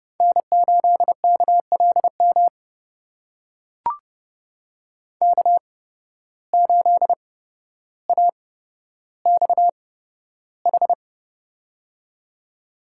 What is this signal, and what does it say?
Morse code, heard as a keyed tone, about 31 wpm, spaced as for 13 wpm, 700 Hz. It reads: N8KLM K8AXH
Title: USA Call Signs - 5 Characters - 8x Character Spacing - 30wpm